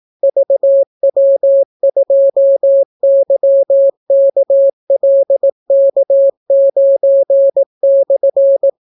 Morse code, heard as a keyed tone, about 18 wpm, 550 Hz. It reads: VW2YKLK9/